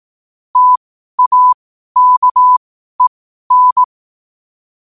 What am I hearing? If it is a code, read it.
Morse code, heard as a keyed tone, about 18 wpm, spaced as for 12 wpm, 1000 Hz. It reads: TAKEN
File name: qotdfiles / 12wpm